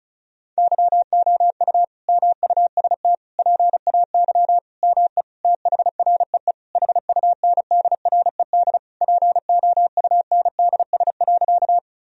Morse code, heard as a keyed tone, about 34 wpm, 700 Hz. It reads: YOU MUST PAY ME THREE HUNDRED POUNDS.
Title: Sentences from Top 1000 Words - Review Part 2 - 35wpm